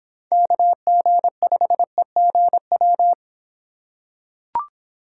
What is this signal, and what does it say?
Morse code, heard as a keyed tone, about 25 wpm, 700 Hz. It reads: KG5EGW